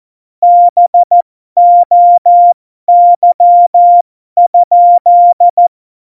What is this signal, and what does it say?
Morse code, heard as a keyed tone, about 14 wpm, spaced as for 12 wpm, 700 Hz. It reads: BOY?
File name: Morsekurs / Witze